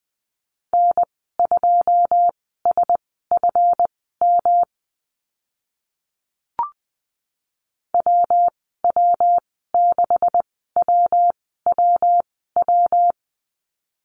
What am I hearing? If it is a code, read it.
Morse code, heard as a keyed tone, about 20 wpm, spaced as for 14 wpm, 700 Hz. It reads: N2SFM WW6WWW
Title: 140 Lesson 23g - Letter f - Call Signs 20wpm